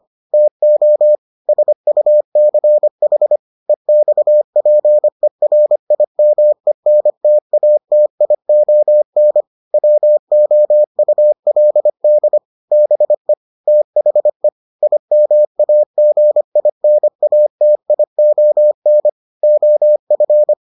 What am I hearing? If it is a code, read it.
Morse code, heard as a keyed tone, about 25 wpm, 600 Hz. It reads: TO SUCH EXPERIMENTATION WOULD BE THE IMAGINATION OF